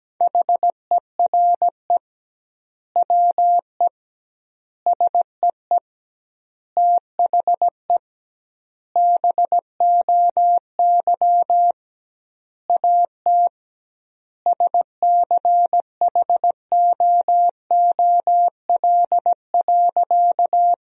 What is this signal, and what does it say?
Morse code, heard as a keyed tone, about 17 wpm, 700 Hz. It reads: HERE WE SEE THE BOY AT SCHOOL.
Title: 2x Word-Spacing - Sentences from Top 300 Words - 17wpm